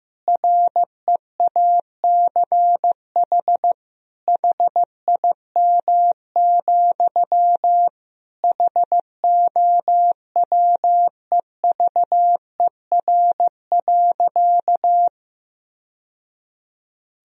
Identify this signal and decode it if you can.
Morse code, heard as a keyed tone, about 15 wpm, 700 Hz. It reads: REACH HIM, HOWEVER.
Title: Sentences from Top 1000 Words - Review Part 2 - 15wpm